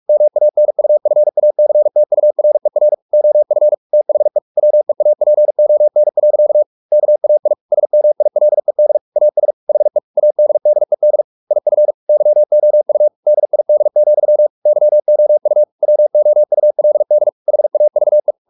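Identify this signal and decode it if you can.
Morse code, heard as a keyed tone, about 44 wpm, 600 Hz. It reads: MANUFACTURER OF THE WEAPON. KAI SMILED AS HE ADDED IF YOU DID, YOU WOULD HAVE
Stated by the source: Morse practice at 45 wpm, the novel Never by Ken Follett in English, from CW med Gnister.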